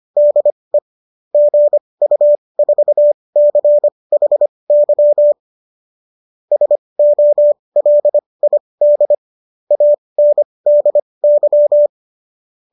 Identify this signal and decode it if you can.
Morse code, heard as a keyed tone, about 25 wpm, spaced as for 20 wpm, 590 Hz. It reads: DE GU4CHY SOLID ANDY